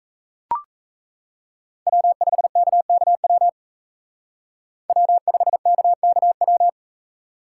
Morse code, heard as a keyed tone, about 36 wpm, 700 Hz. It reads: W5KKW W5KKW